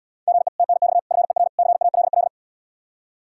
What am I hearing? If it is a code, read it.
Morse code, heard as a keyed tone, about 77 wpm, 700 Hz. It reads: GE TNX FER CALL